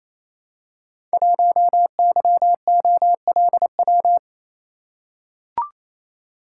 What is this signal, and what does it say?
Morse code, heard as a keyed tone, about 29 wpm, 700 Hz. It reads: 1YOLW
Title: Code Groups - 5 Letter:Numbers - 28wpm